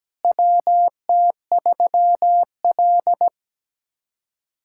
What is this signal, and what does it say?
Morse code, heard as a keyed tone, about 17 wpm, 700 Hz. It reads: WT3L
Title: USA Callsign DE Callsign - 4 Characters - 17wpm